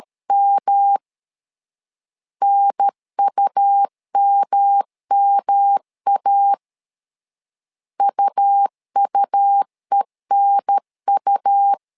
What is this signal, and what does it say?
Morse code, heard as a keyed tone, about 13 wpm, 800 Hz. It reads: M NUMMA UUENU
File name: LEZIONI / 3 MWRO / 3 MWRO 12wpm 1730